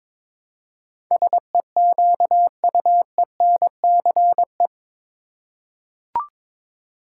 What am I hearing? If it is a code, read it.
Morse code, heard as a keyed tone, about 22 wpm, 700 Hz. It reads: SEQUENCE
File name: practice / 22wpm / Eight Letter Words - 22wpm